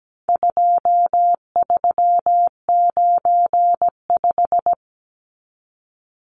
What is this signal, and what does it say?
Morse code, heard as a keyed tone, about 17 wpm, 700 Hz. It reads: 2395
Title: Numbers - 4-Digit 17wpm